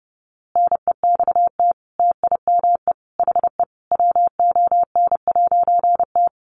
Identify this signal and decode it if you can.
Morse code, heard as a keyed tone, about 30 wpm, 700 Hz. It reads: NEXT TIME HE WON'T
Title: A Sound Of Thunder 30wpm